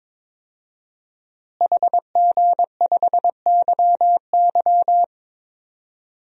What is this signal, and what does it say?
Morse code, heard as a keyed tone, about 22 wpm, 700 Hz. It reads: HG5YY
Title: DXCC-100 Call Signs - 5 Characters - Repeated 3x - 22wpm